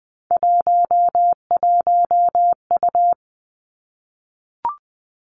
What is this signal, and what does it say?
Morse code, heard as a keyed tone, about 20 wpm, 700 Hz. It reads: 11U
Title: Three Challenging Letter-Number Combinations 20wpm